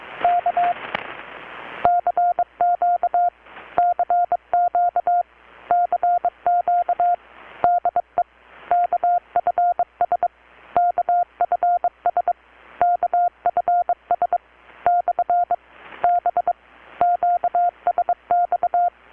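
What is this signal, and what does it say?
Morse code, heard as a keyed tone, about 23 wpm, 720 Hz. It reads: K CQ CQ CQ DE KFS KFS KFS / B QSX
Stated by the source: Morse from KPH's Night of Nights XX maritime radiotelegraphy event, received in Chiapas, Mexico about 3,800 km away on a Kenwood R-600 with a grounded 10 m long-wire antenna.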